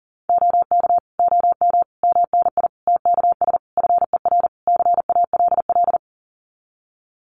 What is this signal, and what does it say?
Morse code, heard as a keyed tone, about 40 wpm, 700 Hz. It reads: OK OM MNI TKS FER CALL
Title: Common QSO Phrases 40wpm